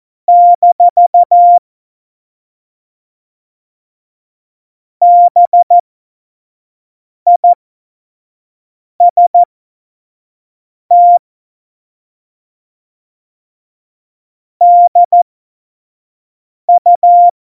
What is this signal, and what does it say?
Morse code, heard as a keyed tone, about 14 wpm, spaced as for 5 wpm, 700 Hz. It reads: - BIST DU
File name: Witz 31. Oktober 17, Tempo 25